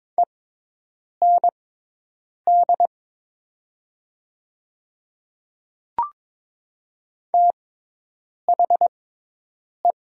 Morse code, heard as a keyed tone, about 22 wpm, spaced as for 8 wpm, 700 Hz. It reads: END THE
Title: 6x Character Spacing - Top 200 Words - 22wpm